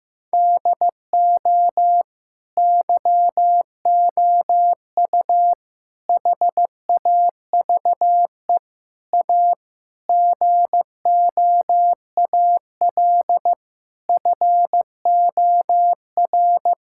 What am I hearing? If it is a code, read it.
Morse code, heard as a keyed tone, about 15 wpm, 700 Hz. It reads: DO YOU HAVE A GOAL FOR
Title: Sentences from Top 1000 Words - Review Part 5 - 15wpm